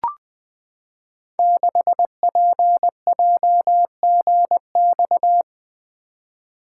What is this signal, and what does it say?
Morse code, heard as a keyed tone, about 20 wpm, 700 Hz. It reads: 6PJGX
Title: Code Groups - 5 Letter:Numbers - 20wpm